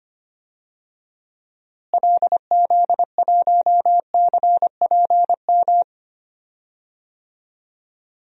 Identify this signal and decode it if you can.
Morse code, heard as a keyed tone, about 25 wpm, 700 Hz. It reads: LZ1CPM
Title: DXCC-100 Call Signs - 6 Characters - Repeated 3x - 25wpm